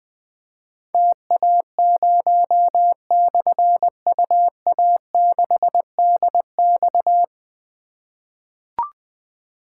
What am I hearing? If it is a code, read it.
Morse code, heard as a keyed tone, about 20 wpm, 700 Hz. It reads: TA0/UA6DX
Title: Difficult Call Signs 20wpm